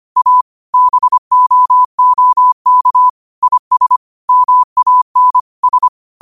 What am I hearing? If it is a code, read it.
Morse code, heard as a keyed tone, about 25 wpm, 1000 Hz. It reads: A BOOK IS MAN S
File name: qotdfiles / 25wpm